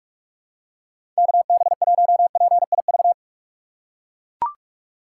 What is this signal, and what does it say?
Morse code, heard as a keyed tone, about 41 wpm, 700 Hz. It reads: KB1PIV
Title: USA Call Signs - 6 Characters - 4x Character Spacing - 45wpm